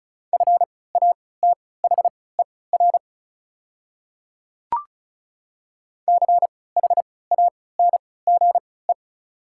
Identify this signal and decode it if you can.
Morse code, heard as a keyed tone, about 36 wpm, spaced as for 20 wpm, 700 Hz. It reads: FATHER CHANGE
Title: 3x Character Spacing - Top 200 Words - 35wpm